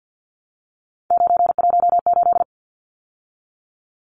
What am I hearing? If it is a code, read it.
Morse code, heard as a keyed tone, about 50 wpm, 700 Hz. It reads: 918
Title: Numbers - 3-Digit 50wpm